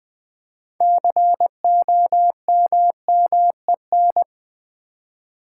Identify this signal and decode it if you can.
Morse code, heard as a keyed tone, about 20 wpm, 700 Hz. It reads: COMMEN
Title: Build A Word - 3 Letters At A Time - 8-Letter Words - 20wpm